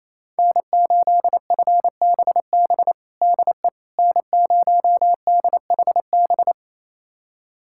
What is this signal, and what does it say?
Morse code, heard as a keyed tone, about 28 wpm, 690 Hz. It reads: N8FBB DE N0DHB
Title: USA Callsign DE Callsign - 5 Characters - 28wpm